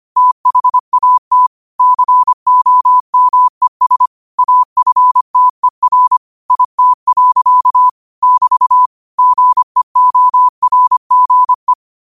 Morse code, heard as a keyed tone, about 25 wpm, 1000 Hz. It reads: THAT COMES AFTER IT. <BT> GEORGE